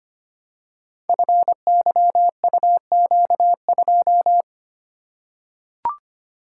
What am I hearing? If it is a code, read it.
Morse code, heard as a keyed tone, about 25 wpm, 700 Hz. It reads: FYUQ2
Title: Code Groups - 5 Letter:Numbers - 25wpm